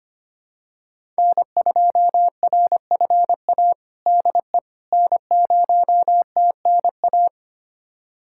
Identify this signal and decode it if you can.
Morse code, heard as a keyed tone, about 25 wpm, 700 Hz. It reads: N2RFA DE N0TNA
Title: USA Callsign DE Callsign - 5 Characters - 25wpm